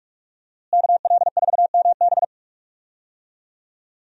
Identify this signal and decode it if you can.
Morse code, heard as a keyed tone, about 44 wpm, 700 Hz. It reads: KL4MB
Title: USA Call Signs - 5 Characters - Repeated 2x - 45wpm